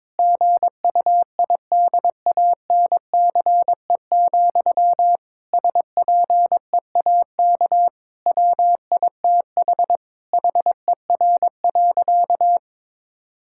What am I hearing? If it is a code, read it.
Morse code, heard as a keyed tone, about 22 wpm, 700 Hz. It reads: GUIDANCE, SPEAK WITH HER.
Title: Sentences from Top 1600 Words 22wpm